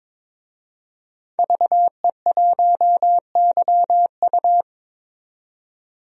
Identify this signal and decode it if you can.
Morse code, heard as a keyed tone, about 21 wpm, 700 Hz. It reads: VE1YU
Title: DXCC-100 Call Signs - 5 Characters - 22wpm